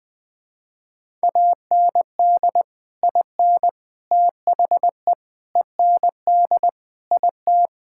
Morse code, heard as a keyed tone, about 21 wpm, 700 Hz. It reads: AND IN THE END IT
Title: Sentences with 3-Word Phrases - 20wpm